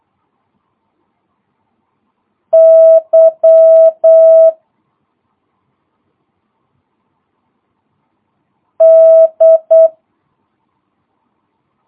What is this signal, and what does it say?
Morse code, heard as a keyed tone, about 8 wpm, 650 Hz. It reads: Y D
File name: jh8cbh / 2022cw